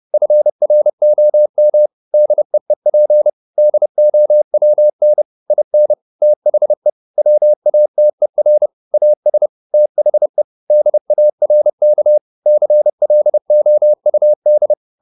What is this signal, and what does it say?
Morse code, heard as a keyed tone, about 30 wpm, 600 Hz. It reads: FROM DEEP DOWN IN THE WATER AS THE DARK CLOUD